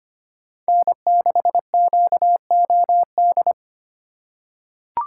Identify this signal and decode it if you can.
Morse code, heard as a keyed tone, about 25 wpm, 700 Hz. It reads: N6QOD
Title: USA Call Signs - 5 Characters - 2x Character Spacing - 25wpm